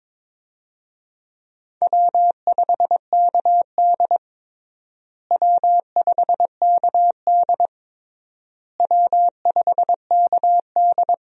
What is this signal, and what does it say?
Morse code, heard as a keyed tone, about 21 wpm, 700 Hz. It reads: W5KD W5KD W5KD
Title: USA Call Signs - 4 Characters - Repeated 3x - 22wpm